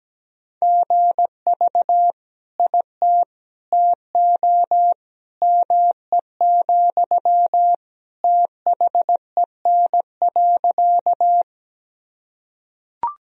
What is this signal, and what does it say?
Morse code, heard as a keyed tone, about 17 wpm, 700 Hz. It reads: GV IT TO ME, THEN.
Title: Sentences from Top 200 Words and Abbreviations 17wpm